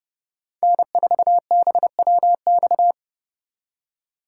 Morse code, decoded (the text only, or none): N4BWX